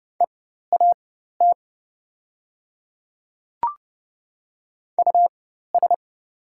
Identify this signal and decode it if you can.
Morse code, heard as a keyed tone, about 30 wpm, spaced as for 14 wpm, 700 Hz. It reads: EAT US
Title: 4x Character Spacing - Top 300 Words - 30wpm